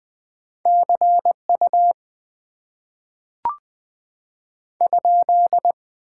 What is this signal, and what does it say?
Morse code, heard as a keyed tone, about 20 wpm, 700 Hz. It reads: CU ?
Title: Top 90 QSO Elements - 6x Character Spacing - 20wpm